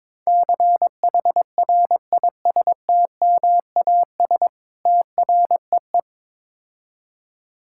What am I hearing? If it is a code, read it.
Morse code, heard as a keyed tone, about 22 wpm, 700 Hz. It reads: CHRISTMAS TREE
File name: Easy Words - Speed-Racing with Hints - 22wpm